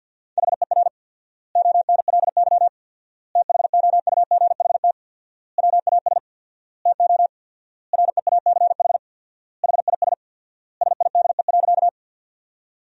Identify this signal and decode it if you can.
Morse code, heard as a keyed tone, about 50 wpm, 700 Hz. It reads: HER ONLY THOUGHT WAS TO REACH HIS SIDE.
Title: 4x Word-Spacing - Sentences from Top 400 Words - 50wpm